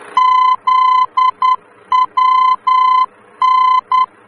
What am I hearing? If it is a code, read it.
Morse code, heard as a keyed tone, about 10 wpm, 1020 Hz. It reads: ZWN
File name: germany / mx / ZWN-D-VOR